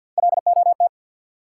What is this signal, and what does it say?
Morse code, heard as a keyed tone, about 52 wpm, 700 Hz. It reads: LOT
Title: Sentences from Top 500 Words - New Word Drill - 50wpm